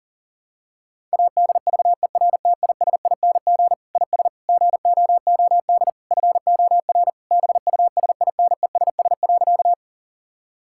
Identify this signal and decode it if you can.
Morse code, heard as a keyed tone, about 37 wpm, 700 Hz. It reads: ADVERTISING IS GOOD FOR BUSINESS.
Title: Sentences from Top 1000 Words - Review Part 6 - 40wpm